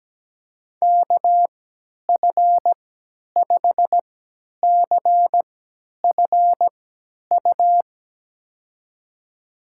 Morse code, heard as a keyed tone, about 17 wpm, spaced as for 10 wpm, 700 Hz. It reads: KF5CFU
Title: USA Call Signs - 6 Characters - 3x Character Spacing - 17wpm